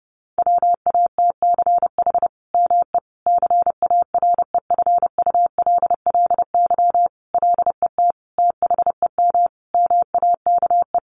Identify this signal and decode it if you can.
Morse code, heard as a keyed tone, about 28 wpm, 700 Hz. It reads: WATCH ME CAREFULLY LET THEM MAKE
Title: Sentences from Top 1300 Words 30wpm